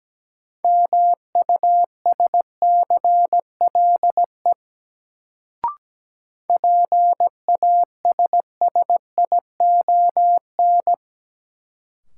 Morse code, heard as a keyed tone, about 17 wpm, 700 Hz. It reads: MUSCLE PASSION
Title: Sentences from Top 1300 Words - New Word Drill Alt - 17wpm